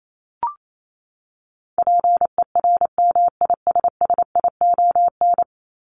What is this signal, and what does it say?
Morse code, heard as a keyed tone, about 28 wpm, 700 Hz. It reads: PERMISSION
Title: Top 1501-1750 Words 28wpm